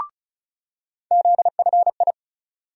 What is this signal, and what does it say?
Morse code, heard as a keyed tone, about 36 wpm, 700 Hz. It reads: ZFI
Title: USA Call Sign Postfixes - 3 Letters 35wpm